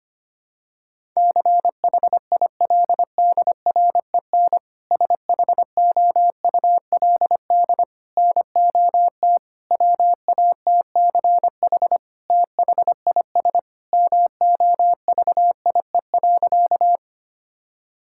Morse code, heard as a keyed tone, about 25 wpm, 700 Hz. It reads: CHILDREN SHOULD NOT WATCH THIS MOVIE.
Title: Sentences from Top 1000 Words - Review Part 5 - 25wpm